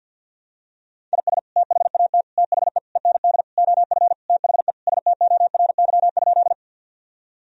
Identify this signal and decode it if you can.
Morse code, heard as a keyed tone, about 52 wpm, 700 Hz. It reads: IS THAT THE END OF THE STORY?
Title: Sentences from Top 1000 Words - Review Part 1 - 50wpm